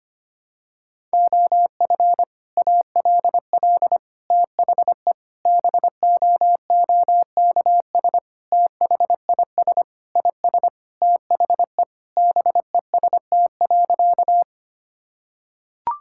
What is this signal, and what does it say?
Morse code, heard as a keyed tone, about 25 wpm, 700 Hz. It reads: OF ALL THE BOOKS THIS IS THE BEST.
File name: practice / 25wpm / Sentences with 3-Word Phrases - 25wpm